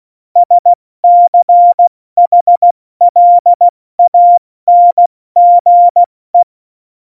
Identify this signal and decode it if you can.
Morse code, heard as a keyed tone, about 16 wpm, 700 Hz. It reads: SCHLANGE
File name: Morsekurs